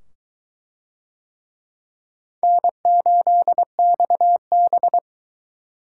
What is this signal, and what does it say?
Morse code, heard as a keyed tone, about 22 wpm, 700 Hz. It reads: N8XB